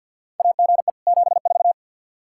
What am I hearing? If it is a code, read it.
Morse code, heard as a keyed tone, about 47 wpm, 700 Hz. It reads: AGE 74
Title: Practice QSO - Exchange 3 - More Info - 50wpm